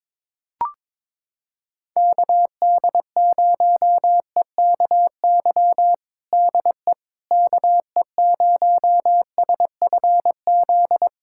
Morse code, heard as a keyed tone, about 22 wpm, 700 Hz. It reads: KD0EKY DE KE0SFZ
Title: USA Callsign DE Callsign - 6 Characters - 22wpm